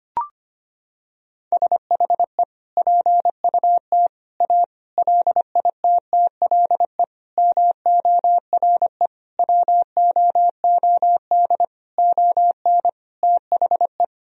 Morse code, heard as a keyed tone, about 25 wpm, 700 Hz. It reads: SHE PUT A LITTLE MORE WOOD ON THE
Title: Sentences from Top 1000 Words - Review Part 2 - 25wpm